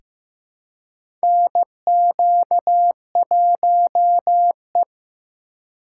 Morse code, heard as a keyed tone, about 15 wpm, 700 Hz. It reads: NQ1E
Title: USA Call Signs - 4 Characters - 2x Character Spacing - 15wpm